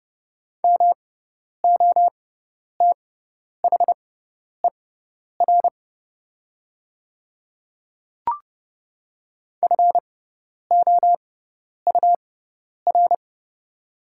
Morse code, heard as a keyed tone, about 30 wpm, spaced as for 10 wpm, 700 Hz. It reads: MOTHER FOUR
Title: 6x Character Spacing - Top 300 Words - 30wpm